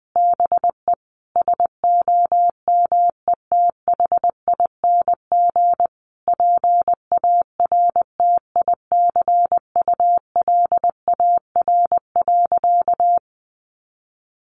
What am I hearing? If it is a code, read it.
Morse code, heard as a keyed tone, about 20 wpm, 700 Hz. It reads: BE SOMETHING PARTICULAR.
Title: Sentences from Top 600 Words 20wpm